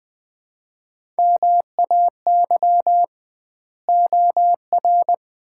Morse code, heard as a keyed tone, about 20 wpm, 700 Hz. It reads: MAY OR
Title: 2x Word-Spacing - Sets of 5 Words - 20wpm